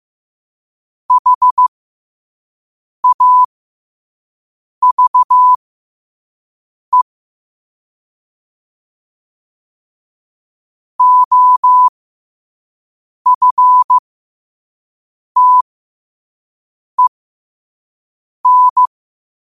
Morse code, heard as a keyed tone, about 15 wpm, spaced as for 5 wpm, 1000 Hz. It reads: HAVE OFTEN
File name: qotdfiles / 5wpm